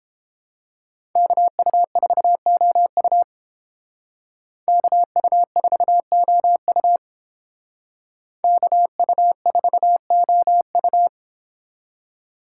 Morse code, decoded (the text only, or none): KU4OU KU4OU KU4OU